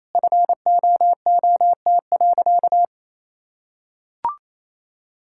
Morse code, heard as a keyed tone, about 28 wpm, 700 Hz. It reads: FOOT.
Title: Sentences from Top 1000 Words - Review Part 6 - 28wpm